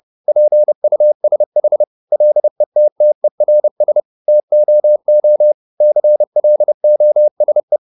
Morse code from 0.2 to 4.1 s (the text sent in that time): PUSH LETTERS